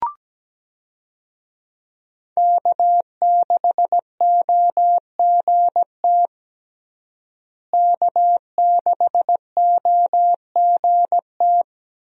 Morse code, heard as a keyed tone, about 17 wpm, 700 Hz. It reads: K6OGT K6OGT
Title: USA Call Signs - 5 Characters - Repeated 3x - 17wpm